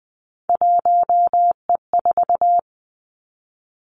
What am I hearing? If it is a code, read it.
Morse code, heard as a keyed tone, about 20 wpm, 700 Hz. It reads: 1E4
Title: Three Challenging Letter-Number Combinations 20wpm